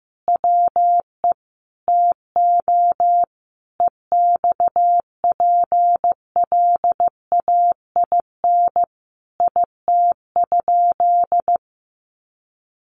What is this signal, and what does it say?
Morse code, heard as a keyed tone, about 15 wpm, 700 Hz. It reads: WE TO EXPLAIN IT?
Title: Sentences from Top 1100 Words 15wpm